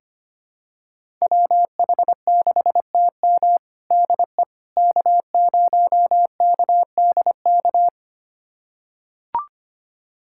Morse code, heard as a keyed tone, about 25 wpm, 700 Hz. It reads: WH6TM DE K0KDK
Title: USA Callsign DE Callsign - 5 Characters - 25wpm